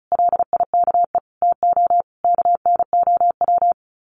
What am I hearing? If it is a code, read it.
Morse code, heard as a keyed tone, about 34 wpm, 700 Hz. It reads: LIKE TO KNOW
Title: Sets of 5 Words Encore 35wpm